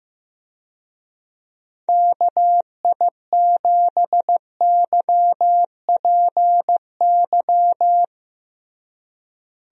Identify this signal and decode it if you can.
Morse code, heard as a keyed tone, about 15 wpm, 700 Hz. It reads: KI7YPY
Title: USA Call Signs - 6 Characters - Repeated 2x - 15wpm